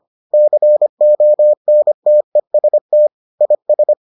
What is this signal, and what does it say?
Morse code, heard as a keyed tone, about 25 wpm, 600 Hz. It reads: CONTEST IS